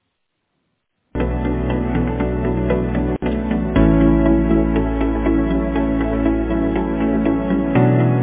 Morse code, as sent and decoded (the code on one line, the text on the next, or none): none